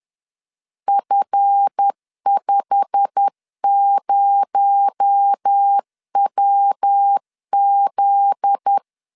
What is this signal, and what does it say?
Morse code, heard as a keyed tone, about 10 wpm, 800 Hz. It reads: F50WZ
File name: LEZIONI / 8 YJ45 / 8 YJ45 10wpm 2549